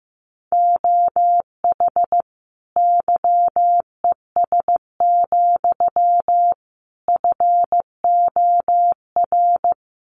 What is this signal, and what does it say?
Morse code, heard as a keyed tone, about 15 wpm, 700 Hz. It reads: OH YES, FOR
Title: Sentences from Top 1300 Words 15wpm